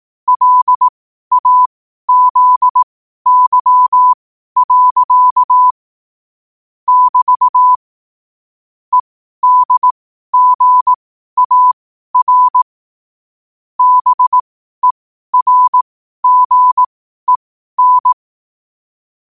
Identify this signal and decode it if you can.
Morse code, heard as a keyed tone, about 18 wpm, spaced as for 12 wpm, 1000 Hz. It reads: LAZY. <BT> EDGAR BERGEN
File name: qotdfiles / 12wpm